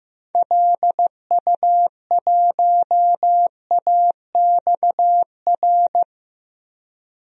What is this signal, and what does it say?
Morse code, heard as a keyed tone, about 15 wpm, 700 Hz. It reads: LU1AXR
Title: DXCC-100 Call Signs - 6 Characters - 15wpm